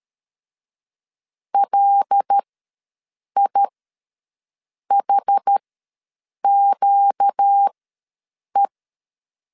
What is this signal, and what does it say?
Morse code, heard as a keyed tone, about 13 wpm, spaced as for 6 wpm, 800 Hz. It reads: LIHQE
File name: LEZIONI / 6 PKX0 / 6 PK ws 12wpm 3116